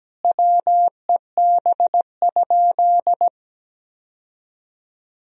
Morse code, heard as a keyed tone, about 17 wpm, 700 Hz. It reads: WEB?
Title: Sentences from Top 1000 Words - Review Part 4 - 17wpm